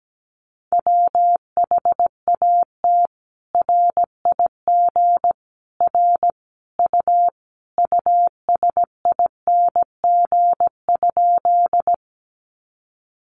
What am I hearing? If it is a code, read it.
Morse code, heard as a keyed tone, about 17 wpm, 700 Hz. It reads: WHAT RIG R U USING?
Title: Common QSO Phrases 17wpm